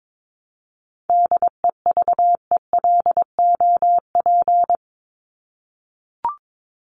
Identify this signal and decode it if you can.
Morse code, heard as a keyed tone, about 22 wpm, 700 Hz. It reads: DEVELOP
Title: ED Suffix - Top 100 Words with Base Words 22wpm